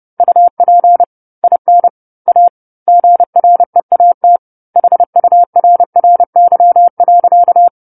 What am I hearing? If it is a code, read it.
Morse code, heard as a keyed tone, about 30 wpm, 700 Hz. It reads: UP IN A GREAT HURRY.
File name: 10-Chapter-10 30 WPM